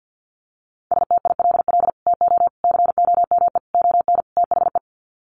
Morse code, heard as a keyed tone, about 50 wpm, 700 Hz. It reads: STILL TO COME ON THE
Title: Sets of 5 Words Encore2 50wpm